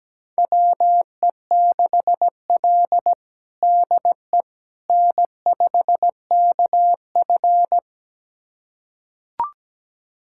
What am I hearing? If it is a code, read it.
Morse code, heard as a keyed tone, about 17 wpm, 700 Hz. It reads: WE6L DE N5KF
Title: USA Callsign DE Callsign - 4 Characters - 17wpm